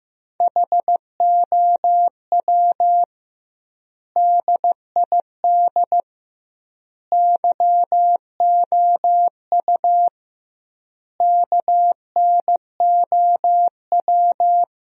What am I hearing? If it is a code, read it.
Morse code, heard as a keyed tone, about 16 wpm, 700 Hz. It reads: HOW DID YOU KNOW
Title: 2x Word-Spacing - Sentences from Top 100 Words - 15wpm